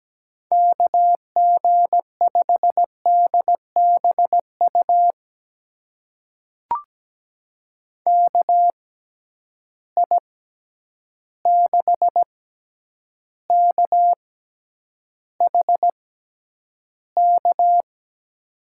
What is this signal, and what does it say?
Morse code, heard as a keyed tone, about 17 wpm, spaced as for 9 wpm, 700 Hz. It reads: KG5DBU KI6KHK
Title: USA Call Signs - 6 Characters - 6x Character Spacing - 17wpm